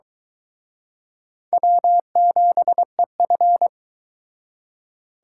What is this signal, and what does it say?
Morse code, heard as a keyed tone, about 23 wpm, 700 Hz. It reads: W7EF